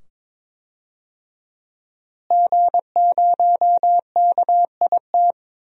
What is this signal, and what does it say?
Morse code, heard as a keyed tone, about 22 wpm, 700 Hz. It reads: G0KIT
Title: DXCC-100 Call Signs - 5 Characters - Repeated 2x - 22wpm